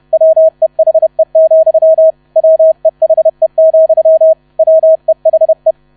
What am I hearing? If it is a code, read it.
Morse code, heard as a keyed tone, about 31 wpm, 640 Hz. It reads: WEHE, WEHE, WEHE
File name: 30wpm